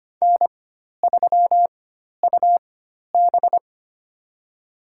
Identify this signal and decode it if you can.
Morse code, heard as a keyed tone, about 25 wpm, spaced as for 12 wpm, 700 Hz. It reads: N3UB
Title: USA Call Signs - 4 Characters - 4x Character Spacing - 25wpm